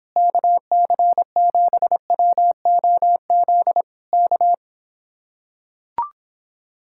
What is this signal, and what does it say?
Morse code, heard as a keyed tone, about 26 wpm, 700 Hz. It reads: KC7WOZ K